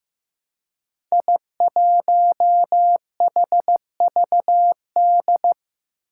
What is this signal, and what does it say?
Morse code, heard as a keyed tone, about 15 wpm, 700 Hz. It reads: I1HVD
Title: Five Challenging Letter-Number Combinations 15wpm